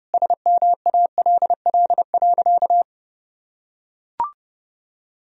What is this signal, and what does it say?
Morse code, heard as a keyed tone, about 30 wpm, 700 Hz. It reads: SMALL.
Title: Sentences from Top 1000 Words - Review Part 1 - 30wpm